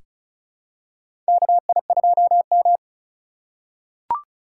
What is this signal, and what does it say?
Morse code, heard as a keyed tone, about 36 wpm, 690 Hz. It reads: KI2M
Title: USA Call Signs - 4 Characters - 2x Character Spacing - 35wpm